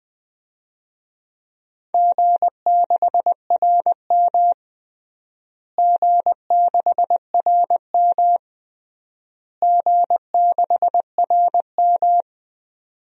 Morse code, heard as a keyed tone, about 20 wpm, 700 Hz. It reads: G6RM G6RM G6RM
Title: DXCC-100 Call Signs - 4 Characters - Repeated 3x - 20wpm